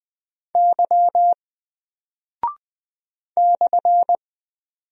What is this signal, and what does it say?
Morse code, heard as a keyed tone, about 20 wpm, 700 Hz. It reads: Y /
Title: Single Character (Letter:Number:Punctuation:Prosign) 20wpm